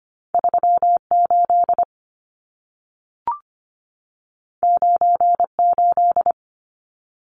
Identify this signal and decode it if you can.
Morse code, heard as a keyed tone, about 25 wpm, 700 Hz. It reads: 38 98
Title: Numbers - 2-Digit 25wpm